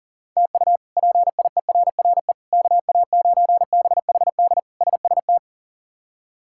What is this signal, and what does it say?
Morse code, heard as a keyed tone, about 40 wpm, 700 Hz. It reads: TU PIERRE KA9BHD SST
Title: K1USN SST QSO Practice - 40wpm